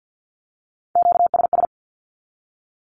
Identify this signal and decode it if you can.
Morse code, heard as a keyed tone, about 48 wpm, 700 Hz. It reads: QSS
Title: Q-Codes 50wpm